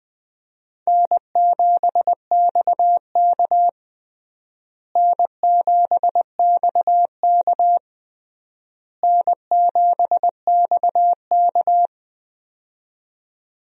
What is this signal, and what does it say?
Morse code, heard as a keyed tone, about 20 wpm, 700 Hz. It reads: N7XK N7XK N7XK
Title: USA Call Signs - 4 Characters - Repeated 3x - 20wpm